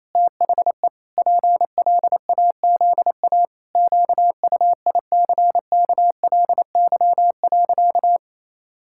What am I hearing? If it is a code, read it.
Morse code, heard as a keyed tone, about 28 wpm, 700 Hz. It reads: THE PLAZA QUICKLY.
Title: Pangram Challenge v2 - 28wpm